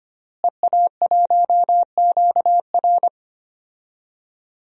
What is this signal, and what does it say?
Morse code, heard as a keyed tone, about 25 wpm, 700 Hz. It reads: EA1QR
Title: DXCC-100 Call Signs - 5 Characters - Repeated 2x - 25wpm